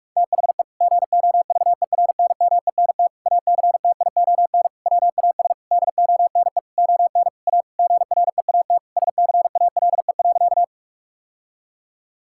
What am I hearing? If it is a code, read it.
Morse code, heard as a keyed tone, about 46 wpm, 700 Hz. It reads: THE GOVERNMENT ACTION WAS DONE ON A GREAT SCALE.
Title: Sentences from Top 1000 Words - Review Part 5 - 45wpm